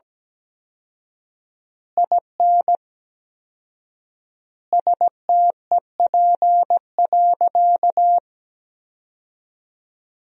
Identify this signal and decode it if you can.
Morse code, heard as a keyed tone, about 17 wpm, 700 Hz. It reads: IN STEP.
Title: 4x Word-Spacing - Sentences from Top 400 Words - 17wpm